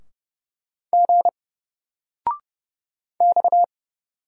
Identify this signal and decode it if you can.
Morse code, heard as a keyed tone, about 31 wpm, 700 Hz. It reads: G X
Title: Single Letter-Number 30wpm